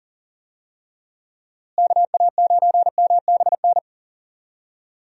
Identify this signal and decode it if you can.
Morse code, heard as a keyed tone, about 40 wpm, 700 Hz. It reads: KA9MBN
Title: USA Call Signs - 6 Characters - Repeated 2x - 40wpm